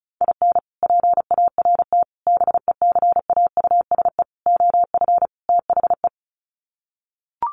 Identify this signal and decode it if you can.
Morse code, heard as a keyed tone, about 34 wpm, 700 Hz. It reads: IN PART BECAUSE OF THE
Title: Sets of 5 Words Encore2 35wpm